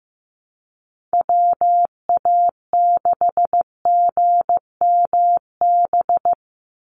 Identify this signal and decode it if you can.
Morse code, heard as a keyed tone, about 15 wpm, 700 Hz. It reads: WA6GMB
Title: USA Call Signs 15wpm